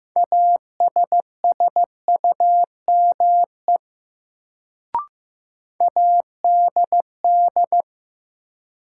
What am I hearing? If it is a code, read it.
Morse code, heard as a keyed tone, about 15 wpm, 700 Hz. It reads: ASSUME ADD
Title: Sentences from Top 1500 Words - New Word Drill Alt - 15wpm